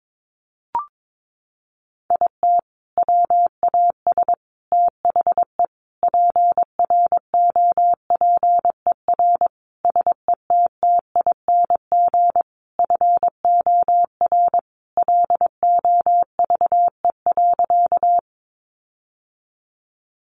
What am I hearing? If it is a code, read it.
Morse code, heard as a keyed tone, about 22 wpm, 700 Hz. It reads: IT WAS THE PROPER SETTING FOR LOVE.
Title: Sentences from Top 1300 Words 22wpm